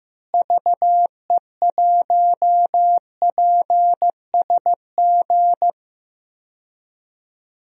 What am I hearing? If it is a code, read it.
Morse code, heard as a keyed tone, about 15 wpm, 700 Hz. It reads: VE1PSG
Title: DXCC-100 Call Signs - 6 Characters - Repeated 3x - 15wpm